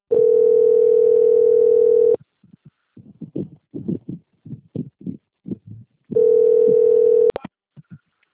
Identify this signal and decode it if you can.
Ringback tone (North America), 440 + 480 Hz.
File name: out- - - - - .22514